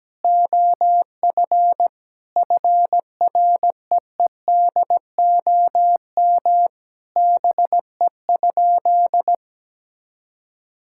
Morse code, heard as a keyed tone, about 17 wpm, 700 Hz. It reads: OF FREEDOM BE?
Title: Sentences from Top 1000 Words - Review Part 5 - 17wpm